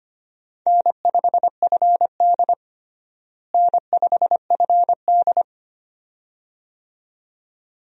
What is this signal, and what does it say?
Morse code, heard as a keyed tone, about 25 wpm, 700 Hz. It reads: N5FD N5FD
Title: USA Call Signs - 4 Characters - Repeated 2x - 25wpm